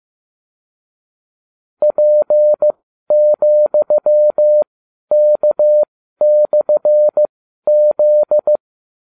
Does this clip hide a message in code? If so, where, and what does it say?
Yes: Morse code from 1.8 to 8.6 s, heard as a keyed tone, about 15 wpm, spaced as for 12 wpm, 600 Hz. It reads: P,K/Z